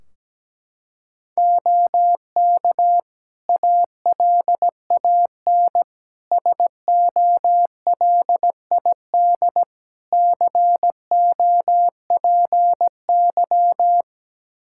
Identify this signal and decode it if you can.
Morse code, heard as a keyed tone, about 17 wpm, 700 Hz. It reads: OK ALAN SOLID COPY
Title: Practice QSO - Exchange 3 - More Info - 17wpm